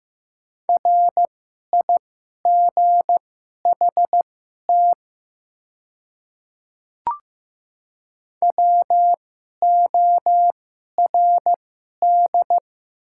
Morse code, heard as a keyed tone, about 15 wpm, spaced as for 11 wpm, 700 Hz. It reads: RIGHT WORD
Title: 2x Character Spacing - Top 200 Words - 15wpm